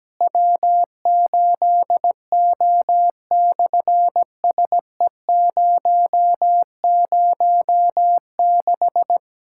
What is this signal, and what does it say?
Morse code, heard as a keyed tone, about 17 wpm, 700 Hz. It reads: W8O/SE006
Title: SOTA QSO Practice 17wpm